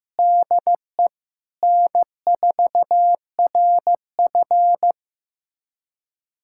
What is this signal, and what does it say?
Morse code, heard as a keyed tone, about 16 wpm, 700 Hz. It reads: DE N4RF
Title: USA Callsign DE Callsign - 4 Characters - 15wpm